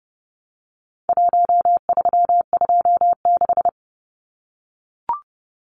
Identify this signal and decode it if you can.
Morse code, heard as a keyed tone, about 30 wpm, 700 Hz. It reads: 1326
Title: Numbers - 4-Digit 30wpm